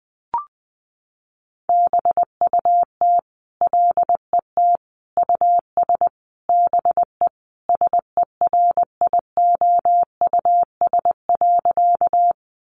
Morse code, heard as a keyed tone, about 20 wpm, 700 Hz. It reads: BUT LET US BE SERIOUS.
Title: Sentences from Top 1100 Words 20wpm